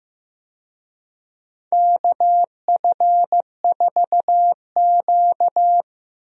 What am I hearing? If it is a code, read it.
Morse code, heard as a keyed tone, about 15 wpm, 700 Hz. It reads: KF4Q
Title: USA Call Signs - 4 Characters - Repeated 2x - 15wpm